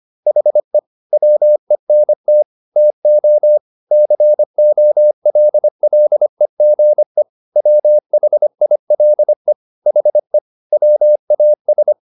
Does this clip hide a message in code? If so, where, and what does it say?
Yes: Morse code from 0.3 to 11.9 s, heard as a keyed tone, about 25 wpm, 600 Hz. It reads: HE WENT TO COLLEGE WHILE HE WAS